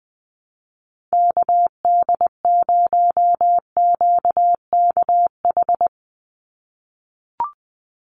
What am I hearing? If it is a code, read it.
Morse code, heard as a keyed tone, about 20 wpm, 700 Hz. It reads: KD0QKH
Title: USA Call Signs 20wpm